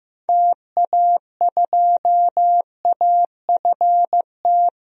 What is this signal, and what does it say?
Morse code, heard as a keyed tone, about 15 wpm, 700 Hz. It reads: TA2AFT
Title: DXCC-100 Call Signs - 6 Characters - 15wpm